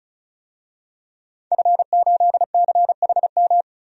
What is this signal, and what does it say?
Morse code, heard as a keyed tone, about 34 wpm, 700 Hz. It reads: F8CHM
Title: DXCC-100 Call Signs - 5 Characters - 35wpm